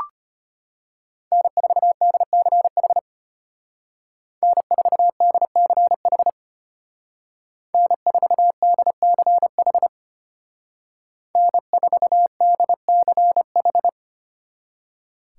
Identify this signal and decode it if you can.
Morse code, heard as a keyed tone, about 33 wpm, 700 Hz. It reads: N4DCH N4DCH N4DCH N4DCH